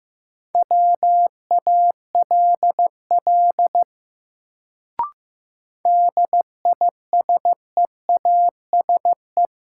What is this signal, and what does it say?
Morse code, heard as a keyed tone, about 15 wpm, 700 Hz. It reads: WALL DISEASE
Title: Sentences from Top 1100 Words - New Word Drill Alt - 15wpm